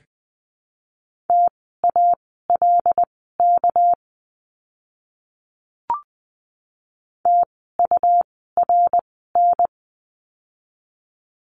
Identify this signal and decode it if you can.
Morse code, heard as a keyed tone, about 19 wpm, spaced as for 14 wpm, 700 Hz. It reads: TALK TURN
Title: 244 Lesson 39f - Letter z - Words 20wpm